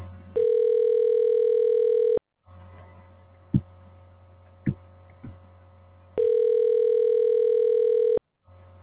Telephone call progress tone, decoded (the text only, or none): ringback tone